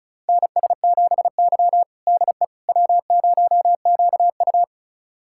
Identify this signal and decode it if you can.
Morse code, heard as a keyed tone, about 36 wpm, 700 Hz. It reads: NS7Y DE W0QU
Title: USA Callsign DE Callsign - 4 Characters - 35wpm